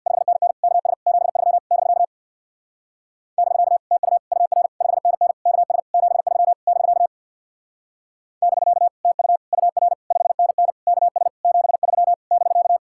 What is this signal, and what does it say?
Morse code, heard as a keyed tone, about 64 wpm, 700 Hz. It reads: 5NN KS 73 <BK> <BK> TU UR 5NN KS 73 <BK> <BK> TU UR 5NN KS 73 <BK>